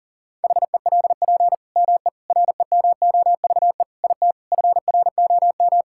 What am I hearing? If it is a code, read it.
Morse code, heard as a keyed tone, about 40 wpm, 700 Hz. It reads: HELP ME REMOVE IT FROM
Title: Sentences from Top 1000 Words - Review Part 5 - 40wpm